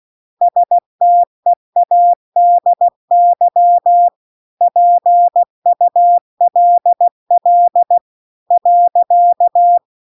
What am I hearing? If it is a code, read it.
Morse code, heard as a keyed tone, about 16 wpm, 700 Hz. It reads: STEADY PULL .
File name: CW-The-Big-Twoheaded-River-2-16WPM